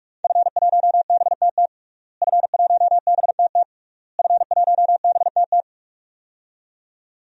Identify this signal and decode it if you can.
Morse code, heard as a keyed tone, about 41 wpm, 700 Hz. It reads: F1BTT F1BTT F1BTT
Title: DXCC-100 Call Signs - 5 Characters - Repeated 3x - 45wpm